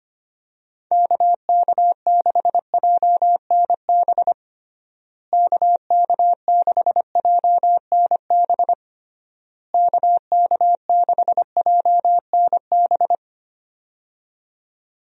Morse code, heard as a keyed tone, about 25 wpm, 700 Hz. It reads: KK6JNB KK6JNB KK6JNB
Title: USA Call Signs - 6 Characters - Repeated 3x - 25wpm